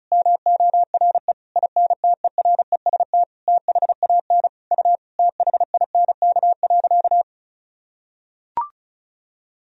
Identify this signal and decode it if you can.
Morse code, heard as a keyed tone, about 34 wpm, 700 Hz. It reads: MORE INTEREST THAN U THINK.
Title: Sentences from Top 400 Words and Abbreviations 35wpm